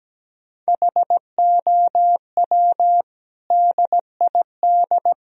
Morse code, heard as a keyed tone, about 17 wpm, 700 Hz. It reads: HOW DID
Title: Sentences from Top 1000 Words - Review Part 2 - 17wpm